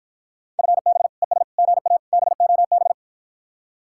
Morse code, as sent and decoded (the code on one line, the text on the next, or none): ..-. -... / . ... / --. .- / -... --- -...
FB ES GA BOB